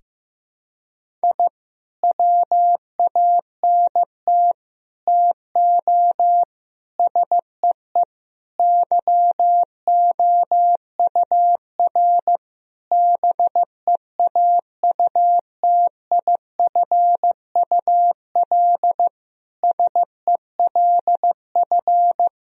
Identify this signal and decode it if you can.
Morse code, heard as a keyed tone, about 15 wpm, 700 Hz. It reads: I WANT TO SEE YOUR BEAUTIFUL SELF
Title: Sentences from Top 1000 Words - Review Part 1 - 15wpm